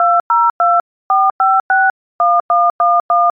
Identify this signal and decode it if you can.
DTMF, heard as telephone touch tones, tones of about 230 ms. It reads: 2024561111